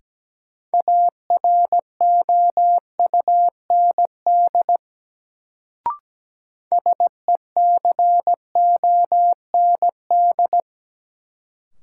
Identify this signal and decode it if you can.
Morse code, heard as a keyed tone, about 17 wpm, 700 Hz. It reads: AROUND SECOND
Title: Sets of 2 Words - New Word Drill - 17wpm